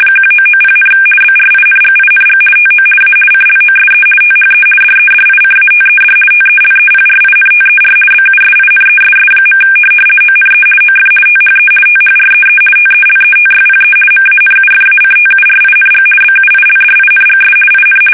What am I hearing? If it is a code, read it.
RTTY, about 50 baud. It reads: 1990, BRINGING TO A CLOSE THE DEDICATED TELEPRINTER BUSINESS. ONE OF THE THREE TELETYPE MANUFACTURING BUILDINGS IN
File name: fsk850-50B / teletype-corp